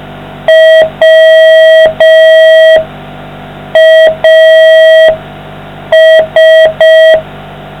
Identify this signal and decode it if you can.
Morse code, heard as a keyed tone, about 5 wpm, 640 Hz. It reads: WAS